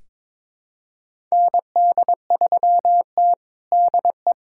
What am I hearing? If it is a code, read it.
Morse code, heard as a keyed tone, about 22 wpm, 700 Hz. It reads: ND3T DE
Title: USA Callsign DE Callsign - 4 Characters - 22wpm